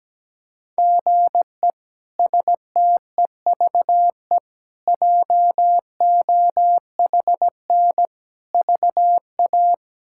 Morse code, heard as a keyed tone, about 17 wpm, 700 Hz. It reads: GE STEVE JOHN VA
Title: K1USN SST QSO Practice - 17wpm